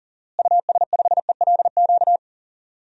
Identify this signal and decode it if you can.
Morse code, heard as a keyed tone, about 40 wpm, 700 Hz. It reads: US5ELQ
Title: DXCC-100 Call Signs - 6 Characters - Repeated 2x - 40wpm